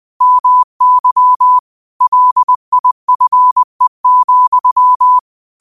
Morse code, heard as a keyed tone, about 20 wpm, 1000 Hz. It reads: MY LIFE,